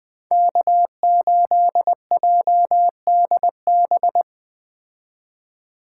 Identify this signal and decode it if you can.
Morse code, heard as a keyed tone, about 20 wpm, 700 Hz. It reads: K8JDB
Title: Speed Racing - USA 1x3 Call Signs 17wpm